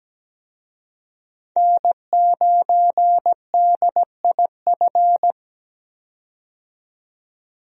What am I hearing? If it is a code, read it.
Morse code, heard as a keyed tone, about 17 wpm, 700 Hz. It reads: N9DIF
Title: USA Call Signs - 5 Characters - 17wpm